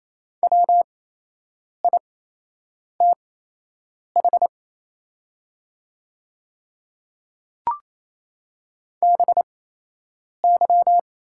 Morse code, heard as a keyed tone, about 28 wpm, spaced as for 8 wpm, 690 Hz. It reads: WITH BY
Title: 8x Character Spacing - Top 200 Words - 28wpm